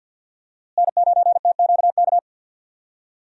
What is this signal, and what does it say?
Morse code, heard as a keyed tone, about 51 wpm, 700 Hz. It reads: N9TQK
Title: USA Call Signs - 5 Characters - Repeated 3x - 50wpm